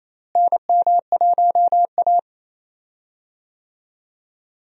Morse code, heard as a keyed tone, about 28 wpm, 700 Hz. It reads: NM1A
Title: USA Call Signs - 4 Characters - Repeated 2x - 28wpm